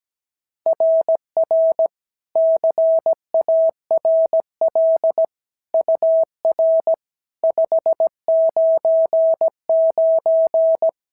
Morse code, heard as a keyed tone, about 17 wpm, 650 Hz. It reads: RR CARL UR 599
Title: SOTA QSO Practice 17wpm